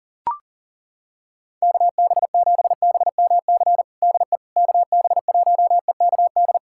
Morse code, heard as a keyed tone, about 40 wpm, 700 Hz. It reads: KB7BMC DE KB1EKD
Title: USA Callsign DE Callsign - 6 Characters - 40wpm